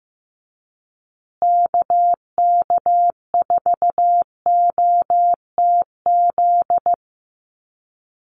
Morse code, heard as a keyed tone, about 15 wpm, 700 Hz. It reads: KK4OTZ
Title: USA Call Signs Repeated 3x 15wpm